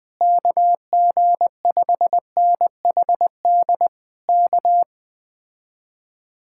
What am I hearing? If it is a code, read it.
Morse code, heard as a keyed tone, about 20 wpm, 700 Hz. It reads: KG5NHD K
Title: Speed-Racing Common QSO Phrases - 17wpm